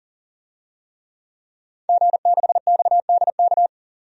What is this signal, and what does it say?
Morse code, heard as a keyed tone, about 40 wpm, 700 Hz. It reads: G6XDK
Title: DXCC-100 Call Signs - 5 Characters - 40wpm